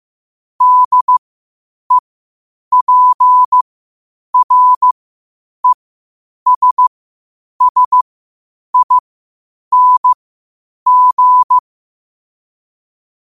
Morse code, heard as a keyed tone, about 15 wpm, spaced as for 8 wpm, 1000 Hz. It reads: DEPRESSING
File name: qotdfiles / 8wpm